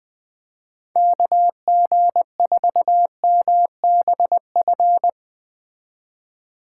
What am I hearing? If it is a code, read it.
Morse code, heard as a keyed tone, about 21 wpm, 700 Hz. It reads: KG4MBF
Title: Speed Racing - USA 2x3 Call Signs 17wpm